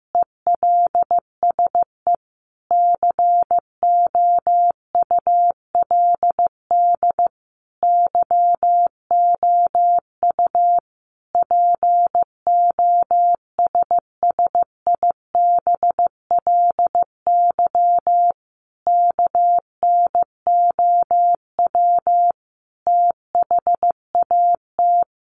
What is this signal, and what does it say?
Morse code, heard as a keyed tone, about 15 wpm, 700 Hz. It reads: ELSE COULD YOU POSSIBLY KNOW THAT
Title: Sentences from Top 1300 Words 15wpm